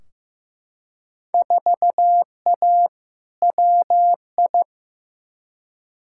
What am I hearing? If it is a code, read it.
Morse code, heard as a keyed tone, about 15 wpm, 700 Hz. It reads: 4A WI
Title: Field Day Exchanges - 15wpm